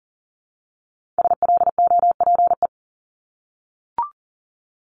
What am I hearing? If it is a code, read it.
Morse code, heard as a keyed tone, about 40 wpm, 700 Hz. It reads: SLOPE
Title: Multi-letter Phonemes 40wpm